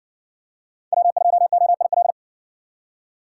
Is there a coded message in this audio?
Yes: Morse code from 0.9 to 2.1 s, heard as a keyed tone, about 58 wpm, 700 Hz. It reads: W2OIL